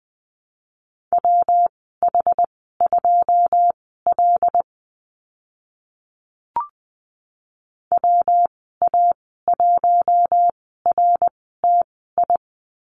Morse code, heard as a keyed tone, about 20 wpm, spaced as for 15 wpm, 700 Hz. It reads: WH2L WA1RTI
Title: 122 Lesson 20g - Number 3 - Call Signs 20wpm